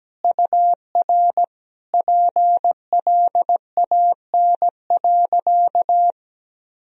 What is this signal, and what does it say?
Morse code, heard as a keyed tone, about 17 wpm, 700 Hz. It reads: UR PLAN.
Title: Sentences from Top 500 Words and Abbreviations 17wpm